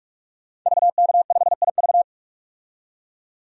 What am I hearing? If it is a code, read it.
Morse code, heard as a keyed tone, about 46 wpm, 690 Hz. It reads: VK5IV
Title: DXCC-100 Call Signs - 5 Characters - 45wpm